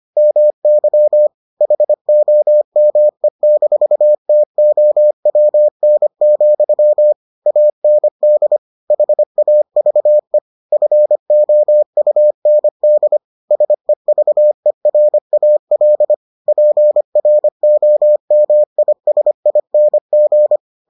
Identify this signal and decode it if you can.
Morse code, heard as a keyed tone, about 25 wpm, 600 Hz. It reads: MY HOME-TOWN, AND HAVE FOUND SEVERAL PROMISING